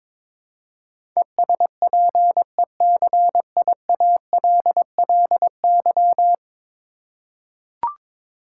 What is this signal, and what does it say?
Morse code, heard as a keyed tone, about 22 wpm, 700 Hz. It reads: ESPECIALLY
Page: morse-code-ninja-practice-22wpm